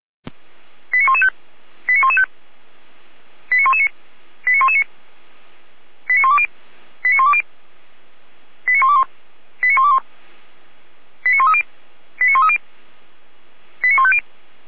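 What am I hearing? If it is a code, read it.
Five-tone selective call (ZVEI), heard as quick runs of five tones, tones of about 70 ms. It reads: 89115 89115 89119 89119 89120 89120 89121 89121 89130 89130 89150
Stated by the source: ZVEI five-tone selective call 89080, received by SDR.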